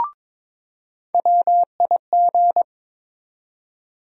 Morse code, heard as a keyed tone, about 22 wpm, 700 Hz. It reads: WIG
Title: USA Call Sign Postfixes - 3 Letters 22wpm